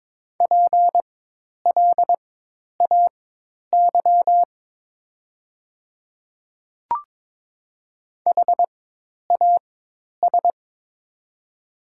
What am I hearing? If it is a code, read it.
Morse code, heard as a keyed tone, about 22 wpm, spaced as for 10 wpm, 700 Hz. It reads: PLAY HAS
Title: 4x Character Spacing - Top 200 Words - 22wpm